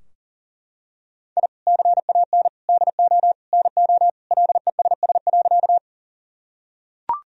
Morse code, heard as a keyed tone, about 37 wpm, 700 Hz. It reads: I CAN DO NO LESS.
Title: Sentences from Top 1000 Words - Review Part 2 - 40wpm